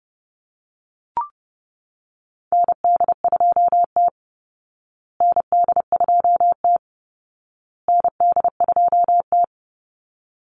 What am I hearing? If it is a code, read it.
Morse code, heard as a keyed tone, about 28 wpm, 700 Hz. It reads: ND2T ND2T ND2T
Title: Call Signs Repeated 2x - 30wpm